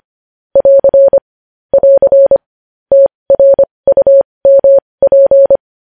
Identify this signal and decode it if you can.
Morse code, heard as a keyed tone, about 25 wpm, spaced as for 20 wpm, 550 Hz. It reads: <AR> <AR> TRUMP
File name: podcast / News headlines-1x-25x20